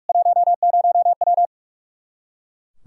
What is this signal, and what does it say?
Morse code, heard as a keyed tone, about 42 wpm, 700 Hz. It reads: 10W
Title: Practice QSO - Exchange 2 - Info - 45wpm